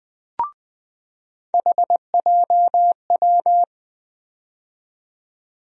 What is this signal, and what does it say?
Morse code, heard as a keyed tone, about 21 wpm, 700 Hz. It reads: HJW
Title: USA Call Sign Postfixes - 3 Letters 20wpm